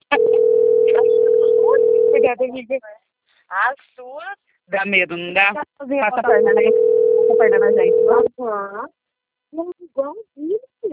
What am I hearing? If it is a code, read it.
Ringback tone (North America), 440 + 480 Hz, with speech.
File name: rg-1600-unknown- - - .51734